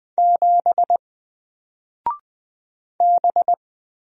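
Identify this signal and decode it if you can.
Morse code, heard as a keyed tone, about 20 wpm, 700 Hz. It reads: 7 B
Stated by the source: Single letter and number Morse drill at 20 wpm, from N4LCD.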